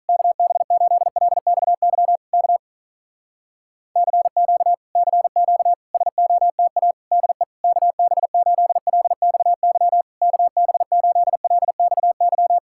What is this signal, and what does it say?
Morse code, heard as a keyed tone, about 48 wpm, 700 Hz. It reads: KB8LXY K CQ CQ SOTA DE KB8LXY KB8LXY